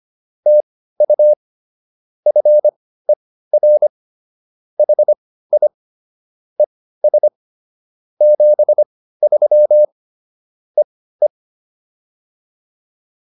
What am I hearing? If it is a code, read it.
Morse code, heard as a keyed tone, about 25 wpm, spaced as for 15 wpm, 600 Hz. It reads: TU FER HI ES 73 EE